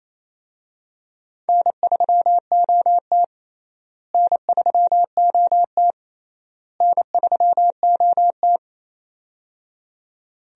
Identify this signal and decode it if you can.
Morse code, heard as a keyed tone, about 28 wpm, 700 Hz. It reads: N3OT N3OT N3OT
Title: USA Call Signs - 4 Characters - Repeated 3x - 28wpm